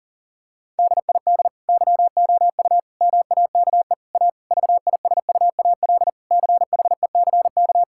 Morse code, heard as a keyed tone, about 40 wpm, 700 Hz. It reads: DID YOU MAKE A VISUAL CHECK
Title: Sentences from Top 1000 Words - Review Part 6 - 40wpm